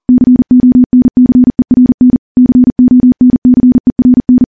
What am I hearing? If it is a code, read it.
Morse code, heard as a keyed tone, about 40 wpm, 250 Hz. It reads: CONCERN CONCERN
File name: morse / morse-training-40WPM